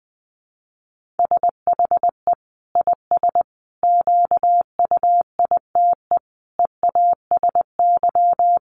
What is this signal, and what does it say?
Morse code, heard as a keyed tone, about 20 wpm, 700 Hz. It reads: SHE IS QUITE EASY
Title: Sentences from Top 1500 Words 20wpm